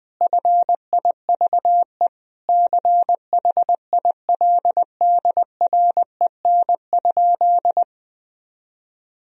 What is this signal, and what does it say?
Morse code, heard as a keyed tone, about 20 wpm, 700 Hz. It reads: FIVE CHILDREN?
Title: Sentences from Top 1000 Words - Review Part 2 - 20wpm